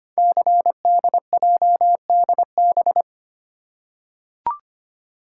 Morse code, heard as a keyed tone, about 25 wpm, 700 Hz. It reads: CDJDB
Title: Code Groups - 5 Letters - 25wpm